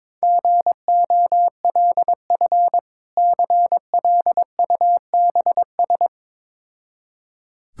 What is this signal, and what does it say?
Morse code, heard as a keyed tone, about 22 wpm, 690 Hz. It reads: GOLF CLUBS
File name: Difficult Words with Hints - 22wpm